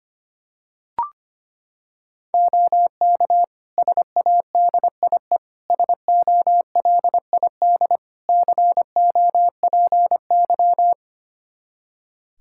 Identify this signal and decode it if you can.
Morse code, heard as a keyed tone, about 25 wpm, 700 Hz. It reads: OK SADIE SOLID COPY
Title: Practice QSO - Exchange 3 - More Info - 25wpm